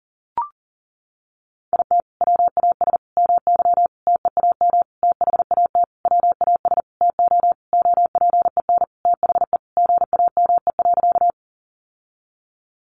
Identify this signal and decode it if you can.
Morse code, heard as a keyed tone, about 40 wpm, 700 Hz. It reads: IT WAS MY TEAM THAT WAS TO OPEN THE GAME.
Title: Sentences from Top 1500 Words 40wpm